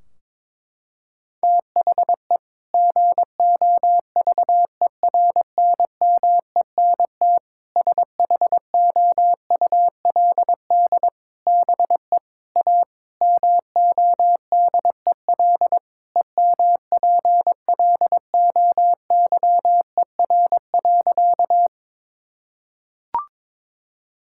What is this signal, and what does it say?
Morse code, heard as a keyed tone, about 22 wpm, 700 Hz. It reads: THE GOVERNMENT SHOULD BE A MODEL EMPLOYER.
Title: Sentences from Top 1600 Words 22wpm